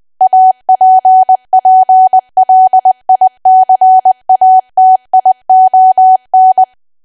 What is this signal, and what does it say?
Morse code, heard as a keyed tone, about 20 wpm, 750 Hz. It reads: APPLICATION
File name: cwa / W1AW-Level3 / 20WPM